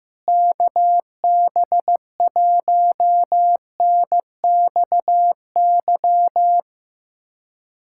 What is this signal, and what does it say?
Morse code, heard as a keyed tone, about 15 wpm, 700 Hz. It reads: KB1NXY
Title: USA Callsign DE Callsign - 6 Characters - 15wpm